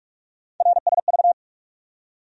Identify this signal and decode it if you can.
Morse code, heard as a keyed tone, about 46 wpm, 700 Hz. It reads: RSV
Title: USA Call Sign Postfixes - 3 Letters 45wpm